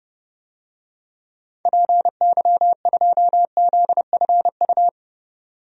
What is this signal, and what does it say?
Morse code, heard as a keyed tone, about 30 wpm, 700 Hz. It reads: PY2ZFU
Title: DXCC-100 Call Signs - 6 Characters - 30wpm